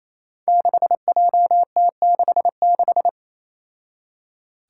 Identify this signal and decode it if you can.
Morse code, heard as a keyed tone, about 27 wpm, 700 Hz. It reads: 6JT66
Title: Five Challenging Letter-Number Combinations 28wpm